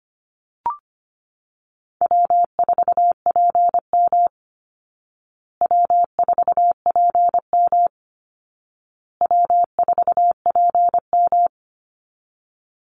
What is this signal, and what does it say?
Morse code, heard as a keyed tone, about 25 wpm, 700 Hz. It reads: W4PM W4PM W4PM
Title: Easy Call Signs Repeated 3x - 25wpm